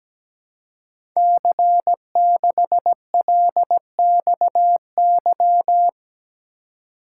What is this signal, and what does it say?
Morse code, heard as a keyed tone, about 17 wpm, 700 Hz. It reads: C6LXY
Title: Code Groups - 5 Letter:Numbers - 17wpm